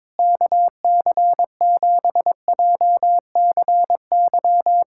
Morse code, heard as a keyed tone, about 22 wpm, 700 Hz. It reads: KC7JCY
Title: USA Callsign DE Callsign - 6 Characters - 22wpm